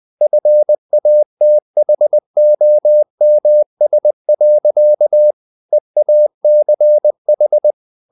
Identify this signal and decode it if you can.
Morse code, heard as a keyed tone, about 20 wpm, 600 Hz. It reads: FATHOMS. EACH
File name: the old man and the sea / the old man and the sea part 1 20wpm